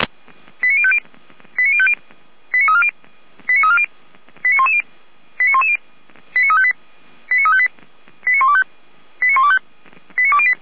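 Five-tone selective call (ZVEI), heard as quick runs of five tones, tones of about 70 ms. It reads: 89050 89050 89340 89340 89110 89110 89358 89358 89125 89125 89208
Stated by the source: ZVEI five-tone call received on an SDR, from feuerwehr-orlamuende fire brigade audio server.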